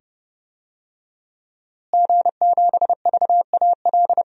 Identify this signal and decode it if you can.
Morse code, heard as a keyed tone, about 30 wpm, 700 Hz. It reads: G7VAL
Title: DXCC-100 Call Signs - 5 Characters - Repeated 2x - 30wpm